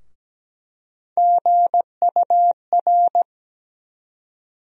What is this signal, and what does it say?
Morse code, heard as a keyed tone, about 17 wpm, 700 Hz. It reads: GUR
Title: USA Call Sign Postfixes - 3 Letters 17wpm